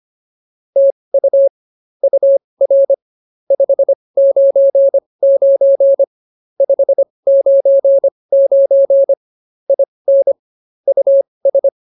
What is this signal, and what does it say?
Morse code, heard as a keyed tone, about 24 wpm, spaced as for 20 wpm, 550 Hz. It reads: TU UR 599 599 IN US